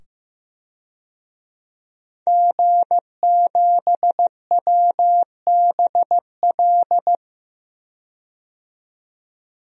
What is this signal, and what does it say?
Morse code, heard as a keyed tone, about 15 wpm, 700 Hz. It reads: G7WBL
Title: DXCC-100 Call Signs - 5 Characters - 15wpm